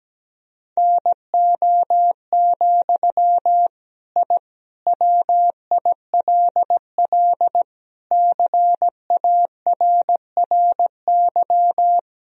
Morse code, heard as a keyed tone, about 17 wpm, 700 Hz. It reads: NO, I WILL CARRY
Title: Sentences from Top 1000 Words - Review Part 1 - 17wpm